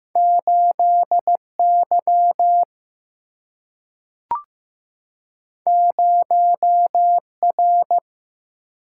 Morse code, heard as a keyed tone, about 15 wpm, 700 Hz. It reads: 8Y 0R
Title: Single-Letter Single-Number 15wpm